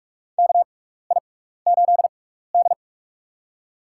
Morse code, heard as a keyed tone, about 43 wpm, spaced as for 15 wpm, 700 Hz. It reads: KI8D
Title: USA Call Signs - 4 Characters - 6x Character Spacing - 45wpm